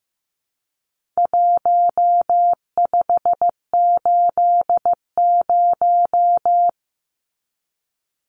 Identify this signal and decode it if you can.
Morse code, heard as a keyed tone, about 15 wpm, 700 Hz. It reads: 1580
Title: Numbers - Contest Style - 4-Digit 15wpm